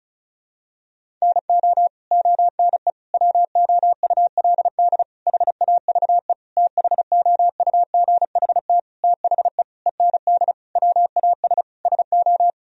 Morse code, heard as a keyed tone, about 36 wpm, 700 Hz. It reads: NO ONE WOULD HAVE THOUGHT THE END WAS SO
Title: Sentences from Top 1000 Words - Review Part 1 - 35wpm